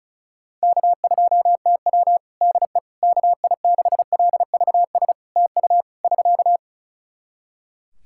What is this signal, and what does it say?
Morse code, heard as a keyed tone, about 36 wpm, 700 Hz. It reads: K2TW DE KI6LVS TU <SK>
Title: Practice QSO - Exchange 4 - Ending - 35wpm